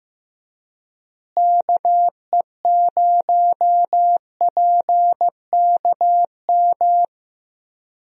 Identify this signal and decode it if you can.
Morse code, heard as a keyed tone, about 15 wpm, 700 Hz. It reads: KE0PKM
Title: USA Call Signs - 6 Characters - Repeated 3x - 15wpm